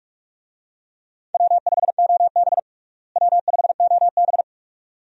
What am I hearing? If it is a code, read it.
Morse code, heard as a keyed tone, about 41 wpm, 700 Hz. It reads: W5OB W5OB
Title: USA Call Signs - 4 Characters - Repeated 2x - 45wpm